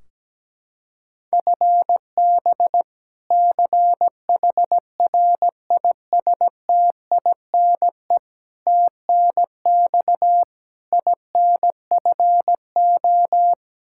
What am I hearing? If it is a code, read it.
Morse code, heard as a keyed tone, about 17 wpm, 700 Hz. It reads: FB CHRISTINE TNX INFO
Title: Practice QSO - Exchange 3 - More Info - 17wpm